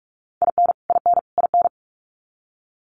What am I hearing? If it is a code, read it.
Morse code, heard as a keyed tone, about 44 wpm, 700 Hz. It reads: IN IN IN
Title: US States Repeated 3x 45wpm